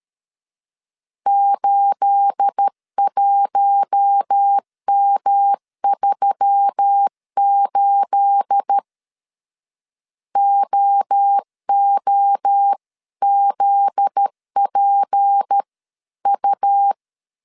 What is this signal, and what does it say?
Morse code, heard as a keyed tone, about 13 wpm, 800 Hz. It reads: 81M38 OOZPU
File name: CW / DET / 1 9 1500C S A 12wpm 3800